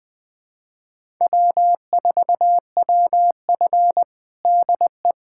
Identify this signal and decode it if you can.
Morse code, heard as a keyed tone, about 20 wpm, 700 Hz. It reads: W4WF DE
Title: USA Callsign DE Callsign - 4 Characters - 20wpm